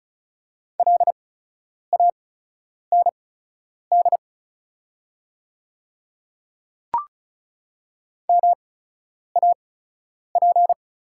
Morse code, heard as a keyed tone, about 36 wpm, spaced as for 10 wpm, 690 Hz. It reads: LAND MAP
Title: 8x Character Spacing - Top 500 Words - 35wpm